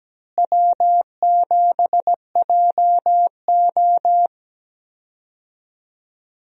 Morse code, heard as a keyed tone, about 17 wpm, 700 Hz. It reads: W7JO
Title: USA Call Signs - 4 Characters - Repeated 2x - 17wpm